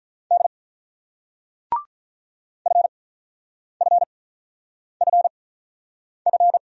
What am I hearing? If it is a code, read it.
Morse code, heard as a keyed tone, about 47 wpm, spaced as for 9 wpm, 700 Hz. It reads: D FFFF